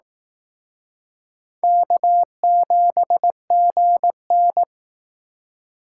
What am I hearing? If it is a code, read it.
Morse code, heard as a keyed tone, about 19 wpm, 700 Hz. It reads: K7GN